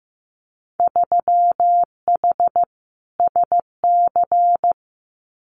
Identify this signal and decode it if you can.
Morse code, heard as a keyed tone, about 15 wpm, 700 Hz. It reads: 3H SC
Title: Winter Field Day Exchanges 15wpm